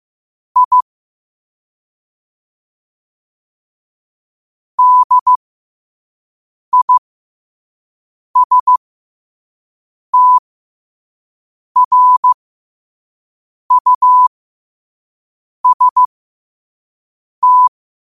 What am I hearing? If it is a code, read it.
Morse code, heard as a keyed tone, about 15 wpm, spaced as for 5 wpm, 1000 Hz. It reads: I DISTRUST
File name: qotdfiles / 5wpm